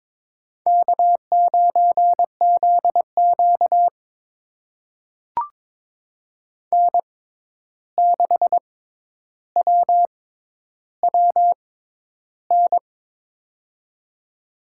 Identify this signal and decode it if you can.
Morse code, heard as a keyed tone, about 22 wpm, spaced as for 11 wpm, 700 Hz. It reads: K9ZQ N6WWN
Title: USA Call Signs - 6x Character Spacing - 22wpm